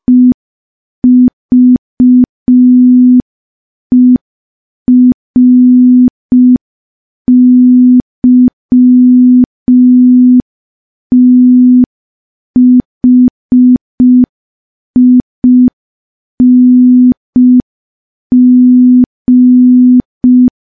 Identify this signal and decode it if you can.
Morse code, heard as a keyed tone, about 5 wpm, 250 Hz. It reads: EVERYTHING
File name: morse / morse-training-05WPM